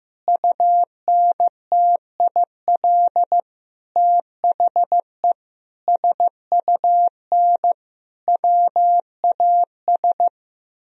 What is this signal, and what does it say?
Morse code, heard as a keyed tone, about 16 wpm, 700 Hz. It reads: UNTIL THE SUN WAS
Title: Sentences from Top 1000 Words - Review Part 2 - 15wpm